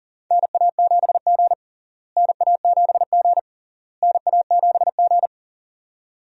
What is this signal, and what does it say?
Morse code, heard as a keyed tone, about 40 wpm, 700 Hz. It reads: NA7G NA7G NA7G
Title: USA Call Signs - 4 Characters - Repeated 3x - 40wpm